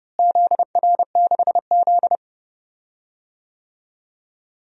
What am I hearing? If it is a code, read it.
Morse code, heard as a keyed tone, about 32 wpm, 700 Hz. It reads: ZR6Z
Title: DXCC-100 Call Signs - 4 Characters - Repeated 3x - 30wpm